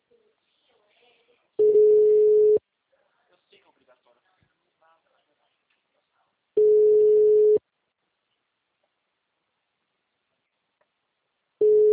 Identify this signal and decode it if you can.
Ringback tone (Europe, 425 Hz), with speech.